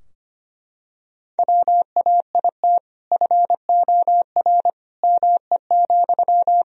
Morse code, heard as a keyed tone, about 25 wpm, 700 Hz. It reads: WAIT FOR ME,
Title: Sentences from Top 1000 Words - Review Part 4 - 25wpm